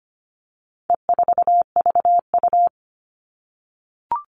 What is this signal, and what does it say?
Morse code, heard as a keyed tone, about 25 wpm, 700 Hz. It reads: E4VU
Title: All Cut Numbers - 4-Digit 25wpm